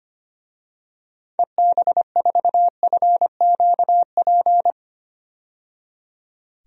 Morse code, heard as a keyed tone, about 25 wpm, 700 Hz. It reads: EB4FQP
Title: DXCC-100 Call Signs - 6 Characters - 25wpm